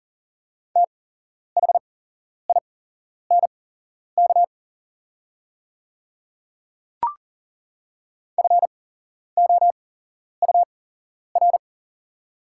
Morse code, heard as a keyed tone, about 40 wpm, spaced as for 11 wpm, 690 Hz. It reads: THINK FOUR
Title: 8x Character Spacing - Top 300 Words - 40wpm